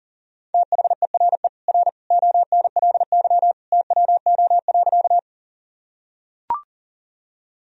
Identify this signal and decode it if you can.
Morse code, heard as a keyed tone, about 40 wpm, 700 Hz. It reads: THERE R ONLY TWO.
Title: Sentences from Top 200 Words and Abbreviations 40wpm